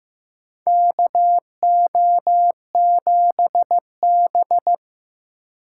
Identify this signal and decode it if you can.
Morse code, heard as a keyed tone, about 16 wpm, 700 Hz. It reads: KO7B
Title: USA Call Signs - 4 Characters - 3x Character Spacing - 15wpm